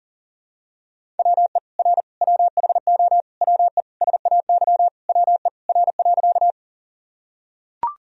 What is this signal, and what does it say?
Morse code, heard as a keyed tone, about 40 wpm, 700 Hz. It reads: WE R WHO WE SAY WE R.
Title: Sentences from Top 200 Words and Abbreviations 40wpm